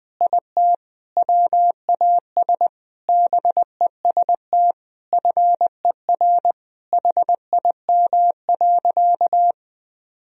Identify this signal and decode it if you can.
Morse code, heard as a keyed tone, about 21 wpm, 700 Hz. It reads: IT WAS BEST FER HIM.
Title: Sentences from Top 500 Words and Abbreviations 20wpm